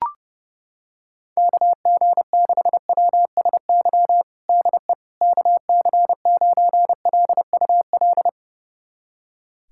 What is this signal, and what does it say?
Morse code, heard as a keyed tone, about 30 wpm, 700 Hz. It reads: KG6WSY DE KC9LUL
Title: USA Callsign DE Callsign - 6 Characters - 30wpm